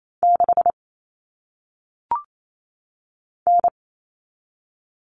Morse code, heard as a keyed tone, about 28 wpm, 700 Hz. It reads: 6 N
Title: Common Cut Numbers - 1-Digit 28wpm